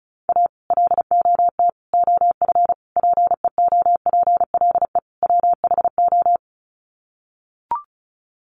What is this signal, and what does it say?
Morse code, heard as a keyed tone, about 34 wpm, 700 Hz. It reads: A LOT OF PEOPLE WHO
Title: Sets of 5 Words 35wpm